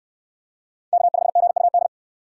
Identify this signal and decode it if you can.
Morse code, heard as a keyed tone, about 68 wpm, 690 Hz. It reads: K5GUD